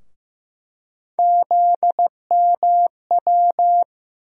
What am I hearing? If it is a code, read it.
Morse code, heard as a keyed tone, about 15 wpm, 700 Hz. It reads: ZMW
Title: USA Call Sign Postfixes - 3 Letters 15wpm